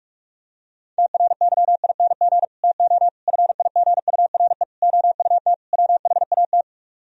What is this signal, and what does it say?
Morse code, heard as a keyed tone, about 50 wpm, 700 Hz. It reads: TRYING TO FIGURE OUT WHAT